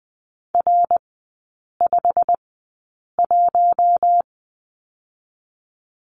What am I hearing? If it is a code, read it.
Morse code, heard as a keyed tone, about 20 wpm, spaced as for 8 wpm, 690 Hz. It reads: R51
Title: 159 Lesson 26e - Letter g - Three Characters 20wpm